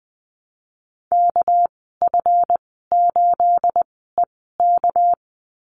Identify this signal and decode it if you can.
Morse code, heard as a keyed tone, about 20 wpm, spaced as for 14 wpm, 700 Hz. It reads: KF8EK
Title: 219 Lesson 35g - Number 0 - Call Signs 20wpm